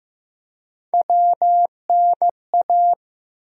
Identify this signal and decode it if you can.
Morse code, heard as a keyed tone, about 15 wpm, 700 Hz. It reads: WNA
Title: USA Call Sign Postfixes - 3 Letters 15wpm